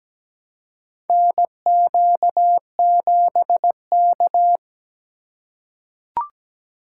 Morse code, heard as a keyed tone, about 17 wpm, 700 Hz. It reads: NQ7K
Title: USA Call Signs - 4 Characters - 3x Character Spacing - 17wpm